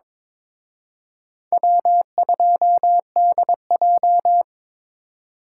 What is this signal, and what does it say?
Morse code, heard as a keyed tone, about 26 wpm, 700 Hz. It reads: W2DJ
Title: Speed Racing - USA 1x2 Call Signs 22wpm